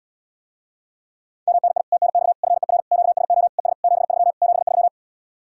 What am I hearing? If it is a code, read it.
Morse code, heard as a keyed tone, about 78 wpm, 700 Hz. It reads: MNI TNX FER QSO ES 73 73